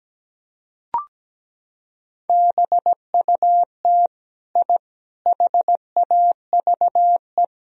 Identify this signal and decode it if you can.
Morse code, heard as a keyed tone, about 17 wpm, 700 Hz. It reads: BUT I HAVE
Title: Sentences from Top 1000 Words - Review Part 4 - 17wpm